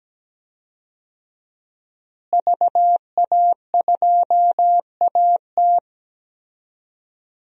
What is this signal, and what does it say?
Morse code, heard as a keyed tone, about 17 wpm, 700 Hz. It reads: VA2AT
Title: DXCC-100 Call Signs - 5 Characters - Repeated 2x - 17wpm